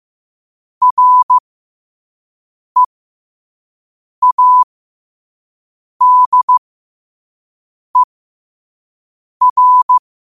Morse code, heard as a keyed tone, about 15 wpm, spaced as for 5 wpm, 1000 Hz. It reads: READER
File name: qotdfiles / 5wpm